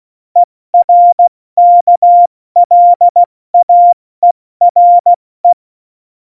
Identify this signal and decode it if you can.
Morse code, heard as a keyed tone, about 16 wpm, 700 Hz. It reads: ERKLAERE
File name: Morsekurs / Witze / Witz 30. Juli 2019, 70 F80